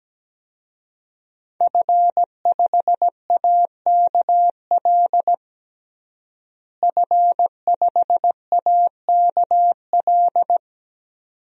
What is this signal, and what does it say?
Morse code, heard as a keyed tone, about 17 wpm, 700 Hz. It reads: F5AKL F5AKL
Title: DXCC-100 Call Signs - 5 Characters - Repeated 2x - 17wpm